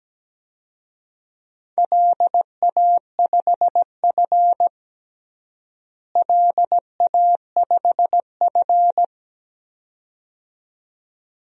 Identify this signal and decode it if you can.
Morse code, heard as a keyed tone, about 17 wpm, 700 Hz. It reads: LA5F LA5F
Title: DXCC-100 Call Signs - 4 Characters - Repeated 2x - 17wpm